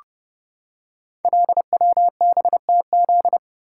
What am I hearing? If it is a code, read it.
Morse code, heard as a keyed tone, about 30 wpm, 700 Hz. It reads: LWBTZ
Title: Code Groups - 5 Letter:Numbers - 30wpm